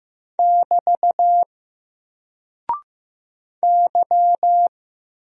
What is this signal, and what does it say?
Morse code, heard as a keyed tone, about 15 wpm, 700 Hz. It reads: <BT> Y
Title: Single Character (Letter:Number:Punctuation:Prosign) 15wpm